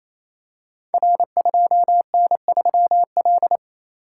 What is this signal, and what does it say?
Morse code, heard as a keyed tone, about 27 wpm, 700 Hz. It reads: R2N3L
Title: Code Groups - 5 Letter:Numbers - 28wpm